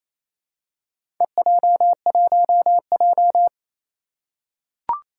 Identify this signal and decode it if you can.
Morse code, heard as a keyed tone, about 27 wpm, 700 Hz. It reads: EJ1J
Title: Four Challenging Letter-Number Combinations 28wpm